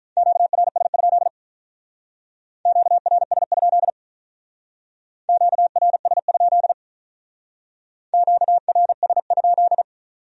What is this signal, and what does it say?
Morse code, heard as a keyed tone, about 44 wpm, 700 Hz. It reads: QRS? QRS? QRS? QRS?